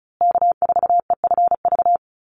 Morse code, heard as a keyed tone, about 34 wpm, 700 Hz. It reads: K4EFV
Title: USA Call Signs Repeated 3x 35wpm